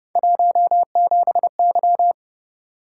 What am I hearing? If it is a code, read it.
Morse code, heard as a keyed tone, about 31 wpm, 700 Hz. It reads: 17Y
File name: Speed-Racing - Three Letter-Number Combinations - 25wpm